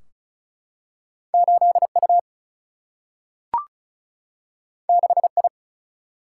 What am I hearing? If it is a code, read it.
Morse code, heard as a keyed tone, about 36 wpm, 700 Hz. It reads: 8U 6I
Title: Single-Letter Single-Number 35wpm